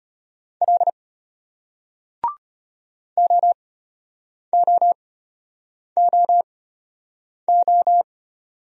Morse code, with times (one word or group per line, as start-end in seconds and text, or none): 0.6-0.9 L
3.2-8.0 OOOO